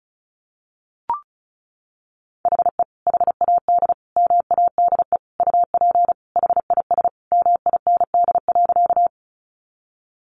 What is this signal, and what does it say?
Morse code, heard as a keyed tone, about 34 wpm, 700 Hz. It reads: HE HAD MADE UP HIS MIND.
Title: Sentences from Top 500 Words 35wpm